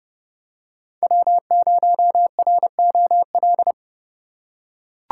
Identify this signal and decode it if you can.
Morse code, heard as a keyed tone, about 29 wpm, 700 Hz. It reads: W0ROL